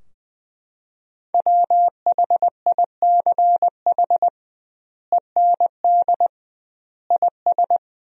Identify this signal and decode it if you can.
Morse code, heard as a keyed tone, about 20 wpm, 700 Hz. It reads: WHICH END IS
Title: 2x Word-Spacing - Sentences from Top 300 Words - 20wpm